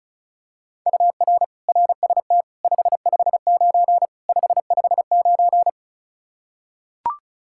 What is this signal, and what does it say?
Morse code, heard as a keyed tone, about 34 wpm, 700 Hz. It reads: UR RST 559 559
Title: K1USN SST QSO Practice 35wpm